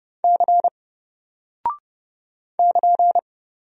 Morse code, heard as a keyed tone, about 30 wpm, 700 Hz. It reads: C <KN>
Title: Single Character (Letter:Number:Punctuation:Prosign) Alt 30wpm